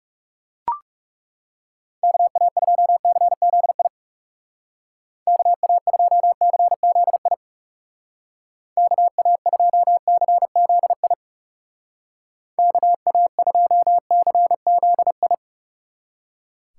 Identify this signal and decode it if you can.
Morse code, heard as a keyed tone, about 36 wpm, 700 Hz. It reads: KA2CZI KA2CZI KA2CZI KA2CZI
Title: Speed Racing - USA 2x3 Call Signs 30wpm